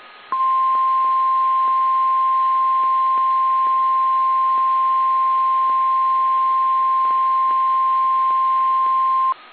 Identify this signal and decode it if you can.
Emergency Alert System (SAME) tones (the 1050 Hz attention tone). It is heard from 0.3 to 9.0 s.